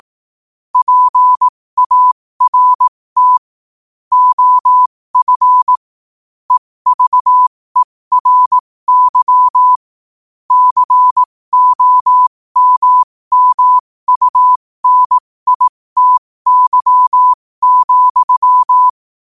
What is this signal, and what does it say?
Morse code, heard as a keyed tone, about 18 wpm, spaced as for 15 wpm, 1000 Hz. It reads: PART OF EVERY COMMUNITY,
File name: qotdfiles / 15wpm